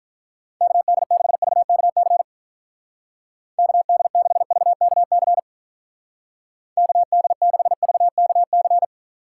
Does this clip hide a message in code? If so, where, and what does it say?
Yes: Morse code from 0.6 to 8.8 s, heard as a keyed tone, about 44 wpm, 700 Hz. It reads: KD6VKC KD6VKC KD6VKC